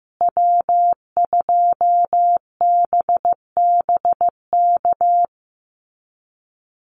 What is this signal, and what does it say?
Morse code, heard as a keyed tone, about 15 wpm, 700 Hz. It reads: W2BBK
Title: USA Call Signs Repeated 2x 15wpm